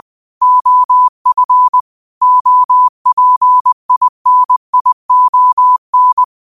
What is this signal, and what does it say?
Morse code, heard as a keyed tone, about 20 wpm, 1000 Hz. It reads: OF OPINION